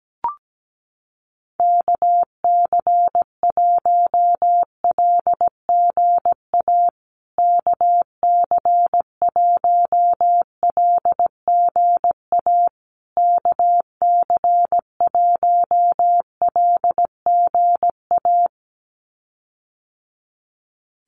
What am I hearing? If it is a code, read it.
Morse code, heard as a keyed tone, about 17 wpm, 700 Hz. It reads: KC1LGA KC1LGA KC1LGA
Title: USA Call Signs Repeated 3x 17wpm